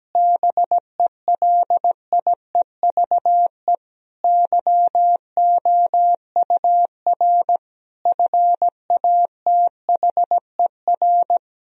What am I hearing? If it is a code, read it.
Morse code, heard as a keyed tone, about 17 wpm, 700 Hz. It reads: BELIEVE YOUR FATHER
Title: Sentences from Top 1000 Words - Review Part 3 - 17wpm